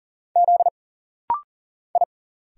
Morse code, heard as a keyed tone, about 40 wpm, 700 Hz. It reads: Z I
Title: Single Letters - Rapid-Fire 40wpm